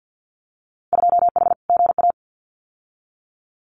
Morse code, heard as a keyed tone, about 50 wpm, 700 Hz. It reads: 2H GA
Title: Winter Field Day Exchanges 50wpm